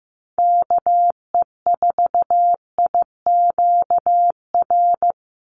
Morse code, heard as a keyed tone, about 15 wpm, 700 Hz. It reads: KE4IQR
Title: USA Call Signs 15wpm